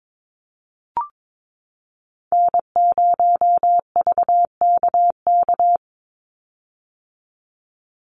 Morse code, heard as a keyed tone, about 22 wpm, 700 Hz. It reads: N0VKK
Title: USA Call Signs 22wpm